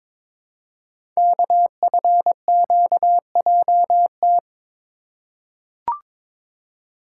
Morse code, heard as a keyed tone, about 22 wpm, 700 Hz. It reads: KFQJT
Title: Code Groups - 5 Letters - 22wpm